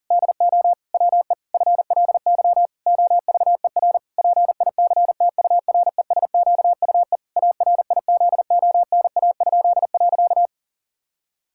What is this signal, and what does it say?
Morse code, heard as a keyed tone, about 40 wpm, 700 Hz. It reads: DO WE FLY OVER PICTURESQUE ARIZONA?.
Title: Pangram Challenge v2 - 40wpm